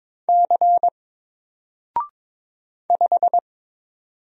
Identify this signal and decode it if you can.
Morse code, heard as a keyed tone, about 22 wpm, 700 Hz. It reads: C 5
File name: Single-Letter-Number-22wpm